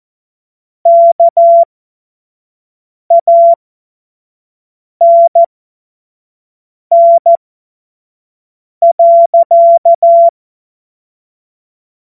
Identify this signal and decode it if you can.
Morse code, heard as a keyed tone, about 14 wpm, spaced as for 5 wpm, 680 Hz. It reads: KANN.
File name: Morsekurs / Witze / Witz 21. Dezember 2021, 25 F70